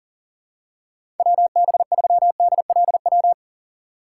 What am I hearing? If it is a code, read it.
Morse code, heard as a keyed tone, about 40 wpm, 700 Hz. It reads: WB3DLW
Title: USA Call Signs - 6 Characters - 6x Character Spacing - 40wpm